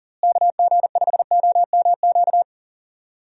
Morse code, heard as a keyed tone, about 40 wpm, 700 Hz. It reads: KG5OMQ
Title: USA Call Signs - 6 Characters - Repeated 2x - 40wpm